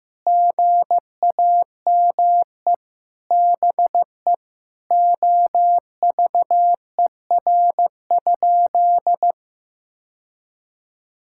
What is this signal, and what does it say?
Morse code, heard as a keyed tone, about 15 wpm, 700 Hz. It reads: GAME BE OVER?
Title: Sentences from Top 1000 Words - Review Part 6 - 15wpm